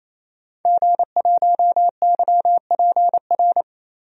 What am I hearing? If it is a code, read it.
Morse code, heard as a keyed tone, about 28 wpm, 700 Hz. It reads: G1YPR
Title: DXCC-100 Call Signs - 5 Characters - 28wpm